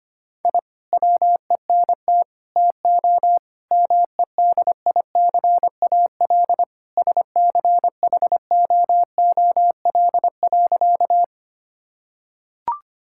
Morse code, heard as a keyed tone, about 25 wpm, 700 Hz. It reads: I WENT TO MEDICAL SCHOOL.
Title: Sentences from Top 1000 Words - Review Part 5 - 25wpm